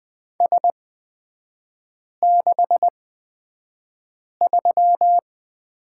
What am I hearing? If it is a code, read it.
Morse code, heard as a keyed tone, about 20 wpm, spaced as for 5 wpm, 700 Hz. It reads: S63
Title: 224 Lesson 36e - BT - Three Characters 20wpm